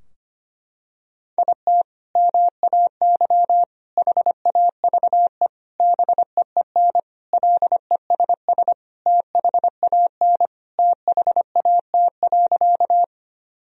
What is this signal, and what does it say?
Morse code, heard as a keyed tone, about 25 wpm, 700 Hz. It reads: IT MAY HAVE BEEN LESS THAN THAT.
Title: Sentences from Top 1000 Words - Review Part 2 - 25wpm